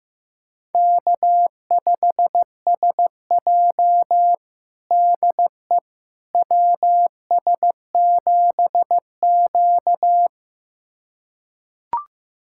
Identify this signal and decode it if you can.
Morse code, heard as a keyed tone, about 15 wpm, 700 Hz. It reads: K5SJ DE WS7Q
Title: USA Callsign DE Callsign - 4 Characters - 15wpm